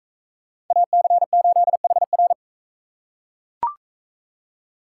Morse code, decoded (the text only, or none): AC8HR